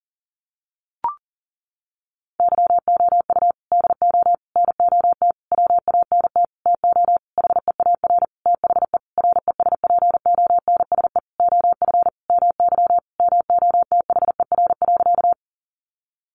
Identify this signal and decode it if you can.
Morse code, heard as a keyed tone, about 40 wpm, 700 Hz. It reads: YOU DO NOT WANT TO HEAR THE RESPONSE OF MY MOTHER.
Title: Sentences from Top 1400 Words 40wpm